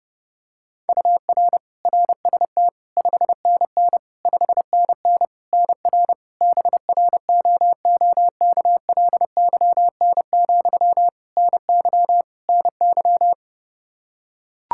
Morse code, heard as a keyed tone, about 30 wpm, 700 Hz. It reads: UR RST 5NN 5NN NR BROOKLYN, NY NY
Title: K1USN SST QSO Practice 30wpm